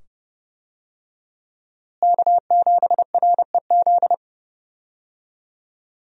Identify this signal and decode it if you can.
Morse code, heard as a keyed tone, about 30 wpm, 700 Hz. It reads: K7REZ
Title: USA Call Signs - 5 Characters - Repeated 2x - 30wpm